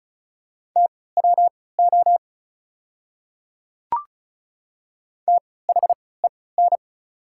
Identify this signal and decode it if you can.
Morse code, heard as a keyed tone, about 34 wpm, spaced as for 20 wpm, 690 Hz. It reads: TWO THEN
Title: 3x Character Spacing - Top 200 Words - 35wpm